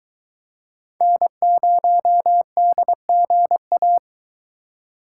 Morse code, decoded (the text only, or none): N0DGA